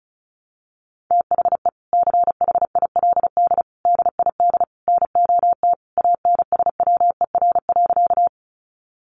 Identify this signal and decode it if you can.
Morse code, heard as a keyed tone, about 34 wpm, 700 Hz. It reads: THE CHILD DID NOT ANSWER.
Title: Sentences from Top 800 Words 35wpm